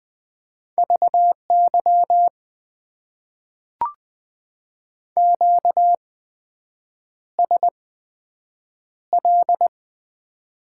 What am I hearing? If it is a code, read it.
Morse code, heard as a keyed tone, about 20 wpm, spaced as for 7 wpm, 700 Hz. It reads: VY QSL
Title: Top 90 QSO Elements - 8x Character Spacing - 20wpm